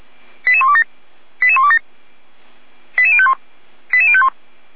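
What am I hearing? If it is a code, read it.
Five-tone selective call (ZVEI), heard as quick runs of five tones, tones of about 70 ms. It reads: 88127 88127 88051 88051
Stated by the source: ZVEI five-tone selective call received by SDR.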